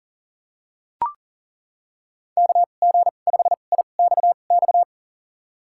Morse code, heard as a keyed tone, about 40 wpm, spaced as for 29 wpm, 700 Hz. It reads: KG5IXX
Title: USA Call Signs - 6 Characters - 2x Character Spacing - 40wpm